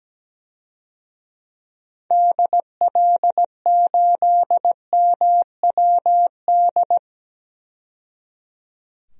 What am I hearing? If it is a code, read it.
Morse code, heard as a keyed tone, about 17 wpm, 700 Hz. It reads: DL8MWD
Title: DXCC-100 Call Signs - 6 Characters - 17wpm